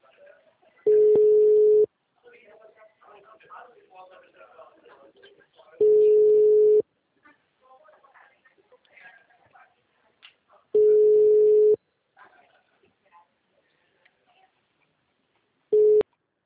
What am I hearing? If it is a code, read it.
Ringback tone (Europe, 425 Hz), with speech.
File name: out- -2021- - - .6